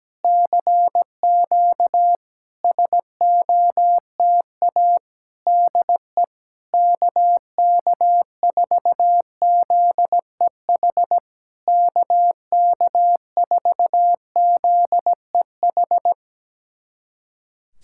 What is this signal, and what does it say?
Morse code, heard as a keyed tone, about 17 wpm, 700 Hz. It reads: CQ SOTA DE KK4ZEH KK4ZEH
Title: SOTA QSO Practice 17wpm